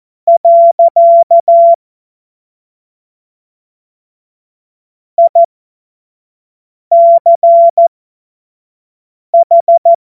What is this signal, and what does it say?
Morse code, heard as a keyed tone, about 14 wpm, spaced as for 5 wpm, 680 Hz. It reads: . ICH
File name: Morsekurs / Witze / Witz 06. Juli 2021, 25 F70